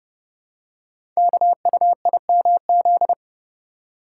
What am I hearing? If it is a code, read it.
Morse code, heard as a keyed tone, about 29 wpm, 700 Hz. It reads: KUIMZ
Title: Code Groups - 5 Letters - 30wpm